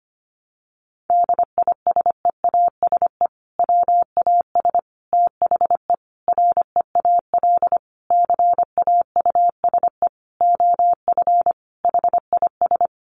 Morse code, heard as a keyed tone, about 25 wpm, 700 Hz. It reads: DISEASE WAS THE REAL CAUSE OF HIS
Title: Sentences from Top 1100 Words 25wpm